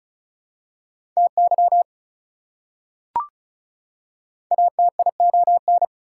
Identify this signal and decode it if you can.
Morse code, heard as a keyed tone, about 36 wpm, 700 Hz. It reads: TY ATION
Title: Prefixes and Suffixes 35wpm